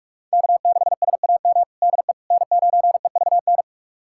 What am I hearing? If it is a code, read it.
Morse code, heard as a keyed tone, about 46 wpm, 700 Hz. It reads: K6SAM DE N9EVN
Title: USA Callsign DE Callsign - 5 Characters - 45wpm